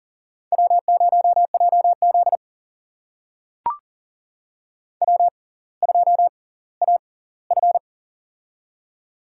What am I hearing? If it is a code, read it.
Morse code, heard as a keyed tone, about 40 wpm, spaced as for 20 wpm, 700 Hz. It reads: W0JZ W2AF
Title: USA Call Signs - 4 Characters - 6x Character Spacing - 40wpm